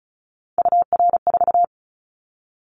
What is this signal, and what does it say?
Morse code, heard as a keyed tone, about 34 wpm, 700 Hz. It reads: UR4
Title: Callsign Prefixes for Top 25 DXCC - Repeated 3x - 35wpm